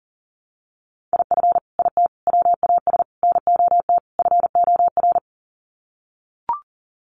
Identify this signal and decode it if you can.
Morse code, heard as a keyed tone, about 40 wpm, 700 Hz. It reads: IF IT WAS NOT FOR
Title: Sets of 5 Words Encore 40wpm